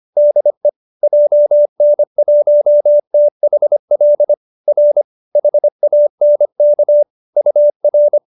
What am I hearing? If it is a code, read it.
Morse code, heard as a keyed tone, about 25 wpm, 590 Hz. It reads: DE JN1THL R HANK UR
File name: QSO305 25